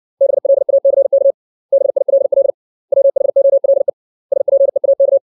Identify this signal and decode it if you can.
Morse code, heard as a keyed tone, about 60 wpm, 550 Hz. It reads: BLACK BILL WHOLE SPEAK
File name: 632 words / 632 Words 60WPM